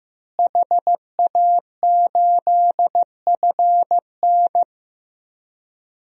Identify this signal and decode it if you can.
Morse code, heard as a keyed tone, about 15 wpm, 700 Hz. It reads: HA8FN
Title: DXCC-100 Call Signs - 5 Characters - Repeated 3x - 15wpm